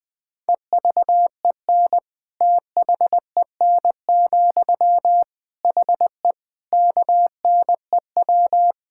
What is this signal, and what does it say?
Morse code, heard as a keyed tone, about 20 wpm, 700 Hz. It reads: EVEN THEN, HE KNEW
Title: Sentences from Top 1600 Words 20wpm